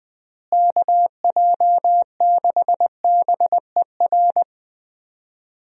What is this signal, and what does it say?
Morse code, heard as a keyed tone, about 20 wpm, 700 Hz. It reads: KJ6BER
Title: USA Call Signs - 6 Characters - Repeated 3x - 20wpm